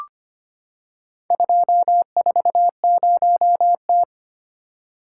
Morse code, heard as a keyed tone, about 25 wpm, 700 Hz. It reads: 240T
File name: practice / 25wpm / Four Letter-Number Combinations - 25wpm